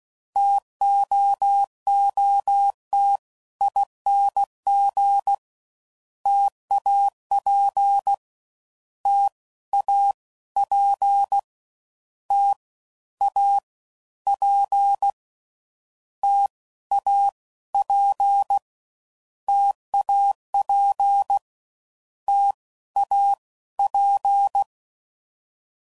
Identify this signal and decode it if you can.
Morse code, heard as a keyed tone, about 16 wpm, 800 Hz. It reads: TOOT ING TAP T A P T A P T A P TAP T A P
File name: Combinations / Difficult Combos / T AOEI's